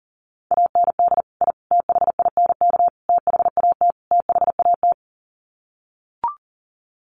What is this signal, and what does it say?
Morse code, heard as a keyed tone, about 40 wpm, 700 Hz. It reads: AND I THINK THAT THAT
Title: Sets of 5 Words Encore 40wpm